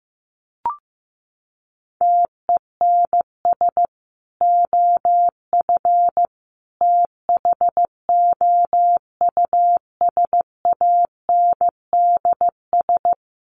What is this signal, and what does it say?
Morse code, heard as a keyed tone, about 15 wpm, 700 Hz. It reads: TENS OF THOUSANDS
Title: Sets of 5 Words Encore2 15wpm